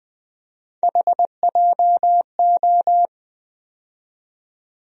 Morse code, heard as a keyed tone, about 20 wpm, 700 Hz. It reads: HJO
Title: USA Call Sign Postfixes - 3 Letters 20wpm